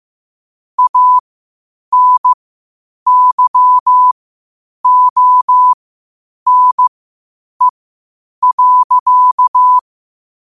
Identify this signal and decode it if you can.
Morse code, heard as a keyed tone, about 15 wpm, spaced as for 8 wpm, 1000 Hz. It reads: ANYONE.
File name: qotdfiles / 8wpm